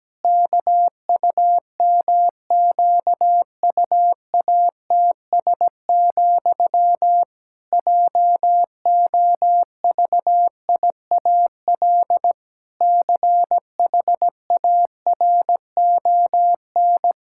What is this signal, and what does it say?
Morse code, heard as a keyed tone, about 17 wpm, 700 Hz. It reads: KUMQUATS, JOVIAL CHARON
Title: Pangram Challenge v2 - 17wpm